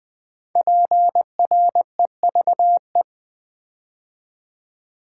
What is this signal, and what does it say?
Morse code, heard as a keyed tone, about 20 wpm, 700 Hz. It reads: PREVE
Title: Build A Word - 5 Letters At A Time - 10-Letter Words - 20wpm